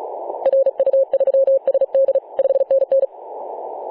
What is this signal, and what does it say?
Morse code, heard as a keyed tone, about 36 wpm, 540 Hz. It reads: RU3SD 5NN
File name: 2013/Sep/22 0824z XZ1Z 24.896MHz CW